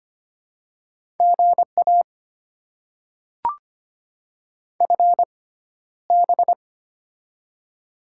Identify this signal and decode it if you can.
Morse code, heard as a keyed tone, about 25 wpm, spaced as for 13 wpm, 700 Hz. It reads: GA FB
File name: Top 90 QSO Elements - 6x Character Spacing - 25wpm